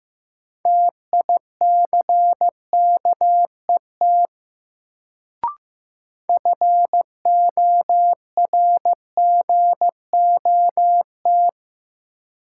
Morse code, heard as a keyed tone, about 15 wpm, 700 Hz. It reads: TICKET FORGOT
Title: Sentences from Top 1400 Words - New Word Drill - 15wpm